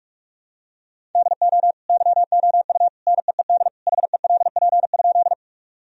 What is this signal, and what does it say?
Morse code, heard as a keyed tone, about 46 wpm, 700 Hz. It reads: DO YOU NEED HELP?
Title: Sentences from Top 1000 Words - Review Part 1 - 45wpm